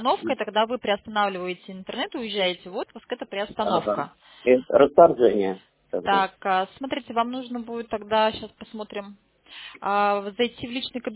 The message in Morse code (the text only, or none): none